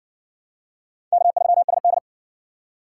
Morse code, heard as a keyed tone, about 50 wpm, 700 Hz. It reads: K3SD
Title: Speed Racing - USA 1x2 Call Signs 40wpm